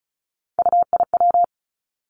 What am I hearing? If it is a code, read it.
Morse code, heard as a keyed tone, about 36 wpm, 700 Hz. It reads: UIW
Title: Three Challenging Letter-Number Combinations 35wpm